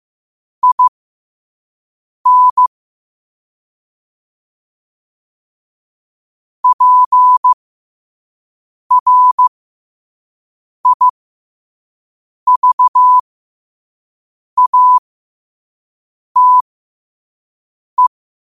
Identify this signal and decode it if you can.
Morse code, heard as a keyed tone, about 15 wpm, spaced as for 5 wpm, 1000 Hz. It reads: IN PRIVATE
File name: qotdfiles / 5wpm